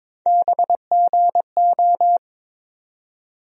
Morse code, heard as a keyed tone, about 22 wpm, 700 Hz. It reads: BGO
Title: USA Call Sign Postfixes - 3 Letters 22wpm